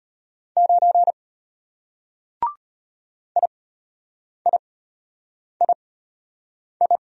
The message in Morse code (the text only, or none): 9 IIII